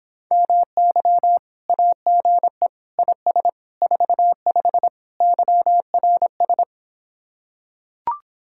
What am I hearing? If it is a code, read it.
Morse code, heard as a keyed tone, about 26 wpm, 700 Hz. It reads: MY AGE IS 45 YRS